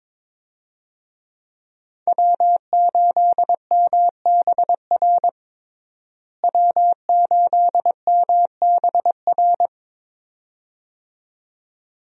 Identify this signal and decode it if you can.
Morse code, heard as a keyed tone, about 22 wpm, 700 Hz. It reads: W8MBR W8MBR
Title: USA Call Signs - 5 Characters - Repeated 2x - 22wpm